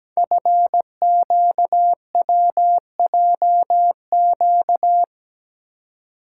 Morse code, heard as a keyed tone, about 17 wpm, 700 Hz. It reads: FQWJQ
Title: Code Groups - 5 Letters - 17wpm